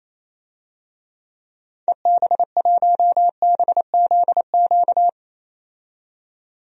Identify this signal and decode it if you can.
Morse code, heard as a keyed tone, about 28 wpm, 700 Hz. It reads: EB1BZQ
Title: DXCC-100 Call Signs - 6 Characters - 28wpm